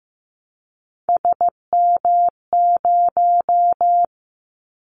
Morse code, heard as a keyed tone, about 15 wpm, 700 Hz. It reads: SM0
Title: Callsign Prefixes for Top 25 DXCC - 15wpm